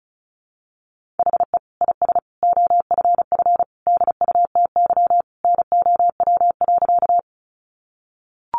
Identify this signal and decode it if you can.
Morse code, heard as a keyed tone, about 34 wpm, 700 Hz. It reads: HE IS OFF DUTY NOW.
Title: Sentences from Top 1500 Words 35wpm